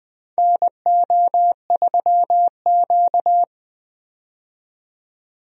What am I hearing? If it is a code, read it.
Morse code, heard as a keyed tone, about 20 wpm, 700 Hz. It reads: NO3Q
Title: USA Call Signs - 4 Characters - 20wpm